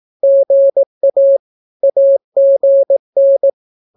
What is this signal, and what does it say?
Morse code, heard as a keyed tone, about 18 wpm, 550 Hz. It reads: GA AGN
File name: naukapodstaw / skroty / skroty